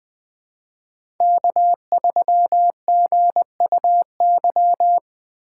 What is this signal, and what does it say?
Morse code, heard as a keyed tone, about 21 wpm, 700 Hz. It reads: K3GUY
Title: USA Call Signs - 6x Character Spacing - 20wpm